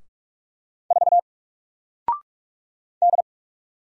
Morse code, heard as a keyed tone, about 48 wpm, 700 Hz. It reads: 4 D
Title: Single Letter-Number Alt 45wpm